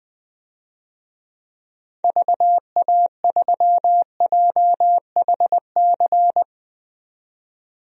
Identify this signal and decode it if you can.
Morse code, heard as a keyed tone, about 20 wpm, 700 Hz. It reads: VA3JHC
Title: DXCC-100 Call Signs - 6 Characters - 20wpm